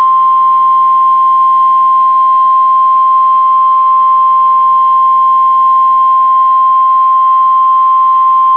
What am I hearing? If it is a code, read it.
Emergency Alert System (SAME) tones (the 1050 Hz attention tone). It is heard from 0.2 to 7.8 s.